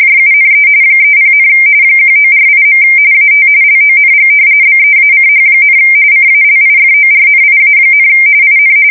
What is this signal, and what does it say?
RTTY, about 45.45 baud. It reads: JUNE 30, 1948. AT LONG BEACH, CALIFORNIA, IN
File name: fsk170 / battleshipnj